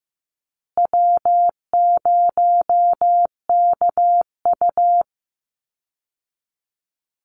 Morse code, heard as a keyed tone, about 15 wpm, 700 Hz. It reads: W0KU
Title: Easy Call Signs Repeated 2x - 15wpm